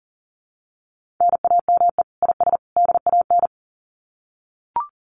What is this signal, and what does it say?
Morse code, heard as a keyed tone, about 40 wpm, 700 Hz. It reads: NAME IS DAN
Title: Common QSO Phrases 40wpm